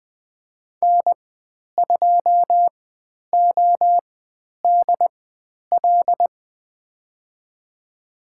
Morse code, heard as a keyed tone, about 20 wpm, spaced as for 10 wpm, 700 Hz. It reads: N2ODL
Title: 095 Lesson 16g - Letter c - Call Signs 20wpm